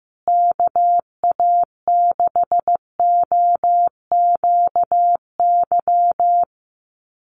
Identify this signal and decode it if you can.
Morse code, heard as a keyed tone, about 15 wpm, 700 Hz. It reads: KA6OQY
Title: USA Call Signs Repeated 2x 15wpm